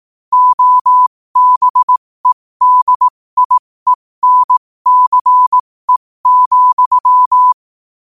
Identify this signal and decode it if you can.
Morse code, heard as a keyed tone, about 18 wpm, spaced as for 15 wpm, 1000 Hz. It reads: OBEDIENCE,
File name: qotdfiles / 15wpm